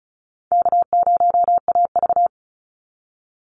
Morse code, heard as a keyed tone, about 34 wpm, 700 Hz. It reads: K0AV
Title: Call Signs - 35wpm